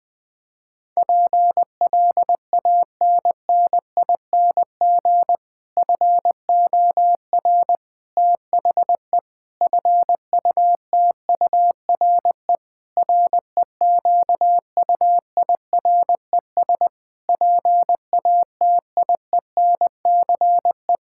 Morse code, heard as a keyed tone, about 20 wpm, 700 Hz. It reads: PLANNING FOR THE FUTURE REQUIRES PATIENCE
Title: Difficult Sentences with Hints - 20wpm